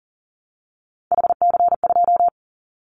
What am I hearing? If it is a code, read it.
Morse code, heard as a keyed tone, about 37 wpm, 700 Hz. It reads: HC2
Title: Callsign Prefixes for Top 100 DXCC - 40wpm